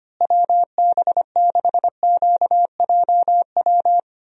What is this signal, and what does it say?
Morse code, heard as a keyed tone, about 25 wpm, 700 Hz. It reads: WB6QJW
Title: USA Call Signs - 6 Characters - Repeated 3x - 25wpm